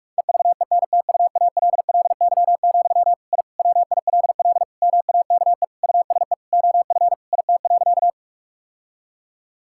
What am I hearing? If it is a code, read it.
Morse code, heard as a keyed tone, about 46 wpm, 700 Hz. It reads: EVENTUALLY, I WILL MAKE USE OF IT.
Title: Sentences from Top 1000 Words - Review Part 6 - 45wpm